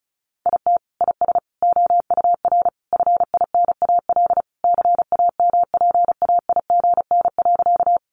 Morse code, heard as a keyed tone, about 34 wpm, 700 Hz. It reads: IT IS OUR FINAL CAMPAIGN.
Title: Sentences from Top 1200 Words 35wpm